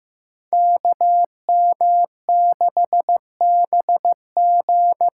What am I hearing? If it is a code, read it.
Morse code, heard as a keyed tone, about 15 wpm, 700 Hz. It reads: KM6BG
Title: USA Call Signs - 6 Characters - 8x Character Spacing - 15wpm